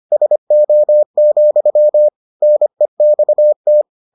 Morse code, heard as a keyed tone, about 25 wpm, 600 Hz. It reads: SO, NEXT